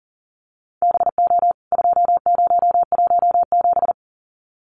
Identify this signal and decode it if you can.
Morse code, heard as a keyed tone, about 40 wpm, 700 Hz. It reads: BO 2017
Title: CWT Exchange 40wpm